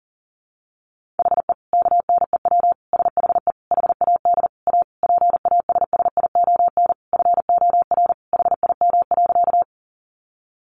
Morse code, heard as a keyed tone, about 40 wpm, 700 Hz. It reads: HE KNEW SHE HAD A PASSION FOR HIM.
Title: Sentences from Top 1300 Words 40wpm